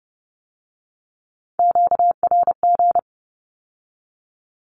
Morse code, heard as a keyed tone, about 30 wpm, 700 Hz. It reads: QRG
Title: Q-Codes with Spoken Meaning - 30wpm